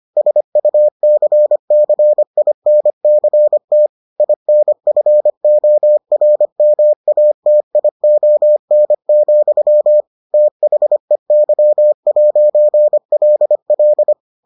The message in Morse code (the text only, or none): SUCCINCT INFORMATION, THEY'LL